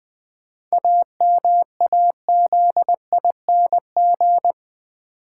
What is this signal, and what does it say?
Morse code, heard as a keyed tone, about 20 wpm, 700 Hz. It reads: AMAZING
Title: Sentences from Top 1500 Words - New Word Drill - 20wpm